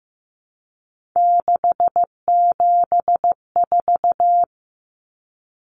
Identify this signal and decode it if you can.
Morse code, heard as a keyed tone, about 15 wpm, 700 Hz. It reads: 674
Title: Common Cut Numbers - 3-Digit 15wpm